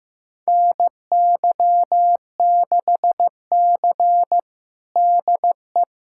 Morse code, heard as a keyed tone, about 16 wpm, 700 Hz. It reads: NY6C DE
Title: USA Callsign DE Callsign - 4 Characters - 15wpm